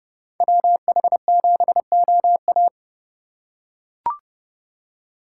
Morse code, heard as a keyed tone, about 29 wpm, 700 Hz. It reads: WH7OA